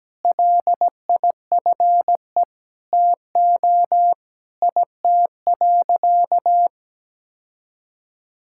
Morse code, heard as a keyed tone, about 17 wpm, 700 Hz. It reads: LIFE TO IT.
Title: Sentences from Top 1000 Words - Review Part 5 - 17wpm